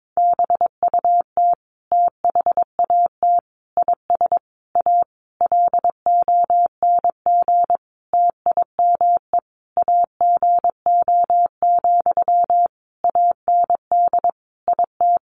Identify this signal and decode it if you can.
Morse code, heard as a keyed tone, about 22 wpm, 700 Hz. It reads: BUT THAT IS A LONG TIME AGO, AND IT
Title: Sentences from Top 600 Words 22wpm